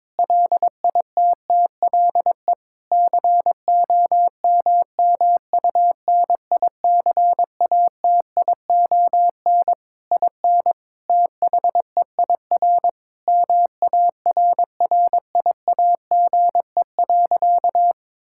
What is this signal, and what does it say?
Morse code, heard as a keyed tone, about 23 wpm, 700 Hz. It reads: LITTLE COMMUNICATION IN THEIR MARRIAGE.
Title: Sentences from Top 1000 Words - Review Part 6 - 22wpm